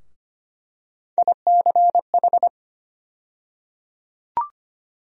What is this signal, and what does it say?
Morse code, heard as a keyed tone, about 25 wpm, 700 Hz. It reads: ICH
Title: USA Call Sign Postfixes - 3 Letters 25wpm